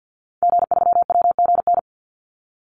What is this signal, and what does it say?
Morse code, heard as a keyed tone, about 50 wpm, 700 Hz. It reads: G3WGN
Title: Call Signs - 50wpm